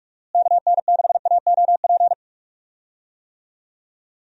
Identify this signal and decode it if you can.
Morse code, heard as a keyed tone, about 41 wpm, 700 Hz. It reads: KN6AOP
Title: USA Call Signs - 6 Characters - Repeated 2x - 45wpm